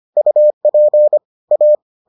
Morse code, heard as a keyed tone, about 25 wpm, 600 Hz. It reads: UP A